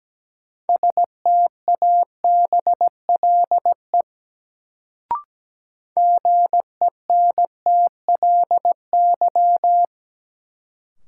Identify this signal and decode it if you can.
Morse code, heard as a keyed tone, about 17 wpm, 700 Hz. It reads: STABLE GENTLY
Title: Sentences from Top 1500 Words - New Word Drill Alt - 17wpm